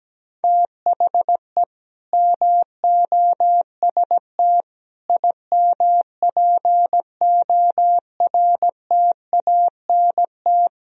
Morse code, heard as a keyed tone, about 17 wpm, 700 Hz. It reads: THE MOST IMPORTANT
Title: Speed-Racing - Sets of 3 Words 17wpm